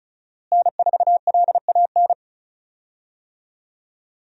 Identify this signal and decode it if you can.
Morse code, heard as a keyed tone, about 34 wpm, 700 Hz. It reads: N4LAN
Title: USA Call Signs - 5 Characters - Repeated 2x - 35wpm